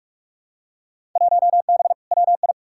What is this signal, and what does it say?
Morse code, heard as a keyed tone, about 45 wpm, 700 Hz. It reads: 1B WI
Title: Field Day Exchanges - 45wpm